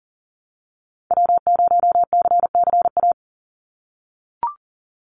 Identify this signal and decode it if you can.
Morse code, heard as a keyed tone, about 40 wpm, 700 Hz. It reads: W0CCA
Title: NA SOTA Activator Callsigns 40wpm